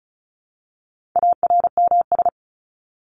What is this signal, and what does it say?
Morse code, heard as a keyed tone, about 34 wpm, 700 Hz. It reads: ARMS
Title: ES:S Suffix - Top 100 Words with Base Words 35wpm